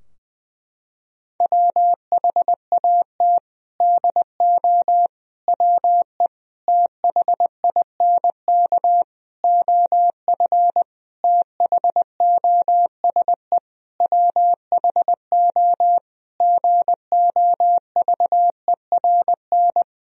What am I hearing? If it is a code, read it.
Morse code, heard as a keyed tone, about 20 wpm, 700 Hz. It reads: WHAT DO WE THINK OF THOSE WHO GOVERN
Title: Sentences from Top 1000 Words - Review Part 2 - 20wpm